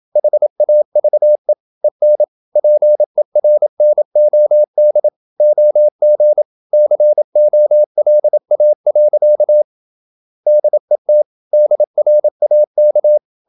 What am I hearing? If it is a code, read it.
Morse code, heard as a keyed tone, about 27 wpm, 600 Hz. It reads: HAVE EN PERNOD OG COLA. DET DRAK
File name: Ferie QRQ 27wpm